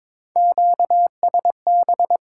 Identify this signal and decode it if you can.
Morse code, heard as a keyed tone, about 22 wpm, 700 Hz. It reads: QSB
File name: Top 90 QSO Elements - 6x Character Spacing - 22wpm